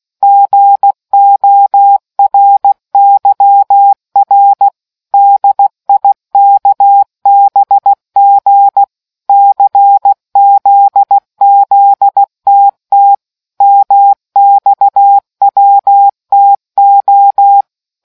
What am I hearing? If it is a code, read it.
Morse code, heard as a keyed tone, about 16 wpm, 800 Hz. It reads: GORYR DIKBG CZZTT MXWTO